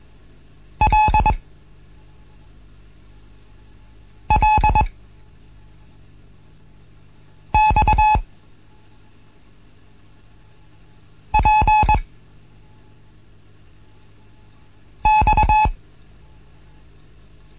Morse code, heard as a keyed tone, about 22 wpm, spaced as for 3 wpm, 860 Hz. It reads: LLXPX
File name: Morsen / Kurs A 15